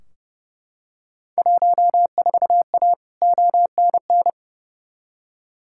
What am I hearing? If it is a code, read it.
Morse code, heard as a keyed tone, about 30 wpm, 700 Hz. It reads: 14A ONN
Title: Field Day Exchanges Encore - 30wpm